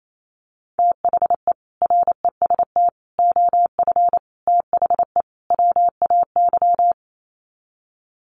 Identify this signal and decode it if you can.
Morse code, heard as a keyed tone, about 28 wpm, 700 Hz. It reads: THE REST OF THE WAY
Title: Sets of 5 Words Encore 28wpm